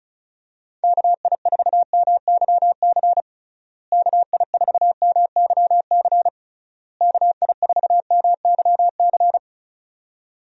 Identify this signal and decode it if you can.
Morse code, heard as a keyed tone, about 34 wpm, 700 Hz. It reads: KI4MYC KI4MYC KI4MYC
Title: USA Call Signs - 6 Characters - Repeated 3x - 35wpm